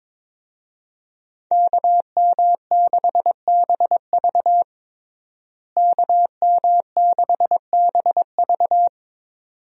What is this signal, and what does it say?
Morse code, heard as a keyed tone, about 22 wpm, 700 Hz. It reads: KM6BV KM6BV
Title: USA Call Signs - 5 Characters - Repeated 2x - 22wpm